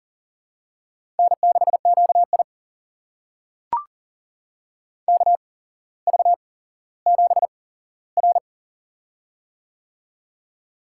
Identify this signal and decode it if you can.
Morse code, heard as a keyed tone, about 42 wpm, spaced as for 17 wpm, 700 Hz. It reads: N6QI KV7R
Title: USA Call Signs - 4 Characters - 8x Character Spacing - 40wpm